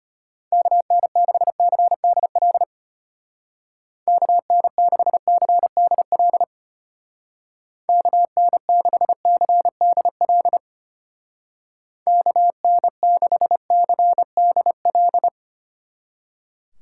Morse code, heard as a keyed tone, about 31 wpm, 700 Hz. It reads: KN6CDL KN6CDL KN6CDL KN6CDL
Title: Speed Racing - USA 2x3 Call Signs 25wpm